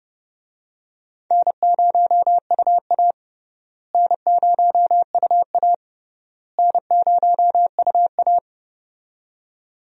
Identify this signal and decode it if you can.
Morse code, heard as a keyed tone, about 30 wpm, 700 Hz. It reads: N0UA N0UA N0UA
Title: USA Call Signs - 4 Characters - Repeated 3x - 30wpm